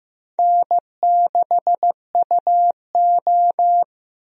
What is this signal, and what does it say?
Morse code, heard as a keyed tone, about 15 wpm, 700 Hz. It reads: N6UO
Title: USA Call Signs - 4 Characters - Repeated 3x - 15wpm